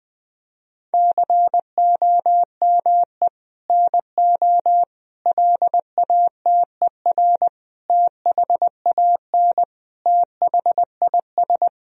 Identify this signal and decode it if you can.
Morse code, heard as a keyed tone, about 20 wpm, 700 Hz. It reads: COME NO LATER THAN THIS
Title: Sentences from Top 1000 Words - Review Part 4 - 20wpm